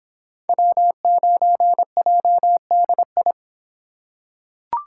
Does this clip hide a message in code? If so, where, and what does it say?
Yes: Morse code from 0.5 to 3.3 s, heard as a keyed tone, about 25 wpm, 700 Hz. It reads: W9JDI